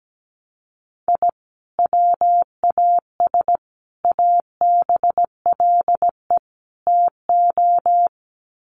Morse code, heard as a keyed tone, about 17 wpm, 700 Hz. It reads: I WAS ABLE TO
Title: Sets of 4 Words 17wpm